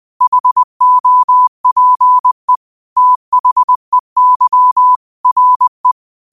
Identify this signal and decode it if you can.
Morse code, heard as a keyed tone, about 20 wpm, 1000 Hz. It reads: HOPE THEY RE
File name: qotdfiles / 20wpm